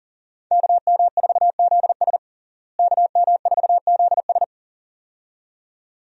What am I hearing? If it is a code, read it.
Morse code, heard as a keyed tone, about 40 wpm, 700 Hz. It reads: KM4ZS KM4ZS
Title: USA Call Signs - 5 Characters - Repeated 2x - 40wpm